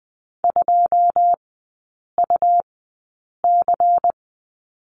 Morse code, heard as a keyed tone, about 20 wpm, spaced as for 8 wpm, 690 Hz. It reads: 2UC
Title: 166 Lesson 27e - Number 7 - Three Characters 20wpm